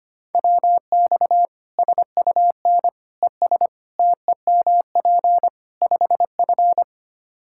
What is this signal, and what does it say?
Morse code, heard as a keyed tone, about 25 wpm, 700 Hz. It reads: WX SUN ES TEMP 5F
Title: Practice QSO - Exchange 2 - Info - 25wpm